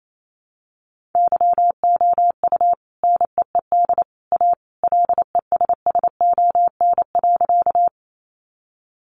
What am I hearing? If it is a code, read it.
Morse code, heard as a keyed tone, about 28 wpm, 700 Hz. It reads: YOU NEED A LESSON.
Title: Sentences from Top 1200 Words 28wpm